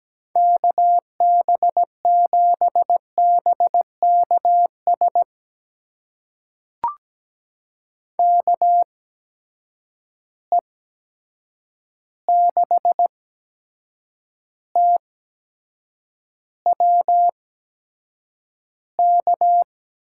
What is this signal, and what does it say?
Morse code, heard as a keyed tone, about 17 wpm, spaced as for 7 wpm, 700 Hz. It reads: KB7BKS KE6TWK
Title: USA Call Signs - 8x Character Spacing - 17wpm